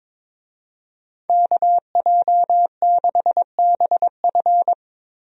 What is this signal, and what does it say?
Morse code, heard as a keyed tone, about 22 wpm, 700 Hz. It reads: KJ6BF
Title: USA Call Signs - 5 Characters - Repeated 2x - 22wpm